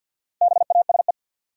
Morse code, heard as a keyed tone, about 52 wpm, 700 Hz. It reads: BASE
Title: Sentences from Top 1000 Words - Review Part 1 - New Word Drill - 50wpm